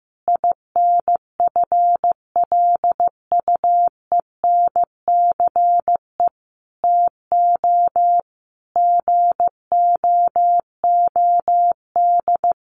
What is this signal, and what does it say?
Morse code, heard as a keyed tone, about 15 wpm, 700 Hz. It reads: INFLUENCE TO GOOD
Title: Sentences from Top 1100 Words 15wpm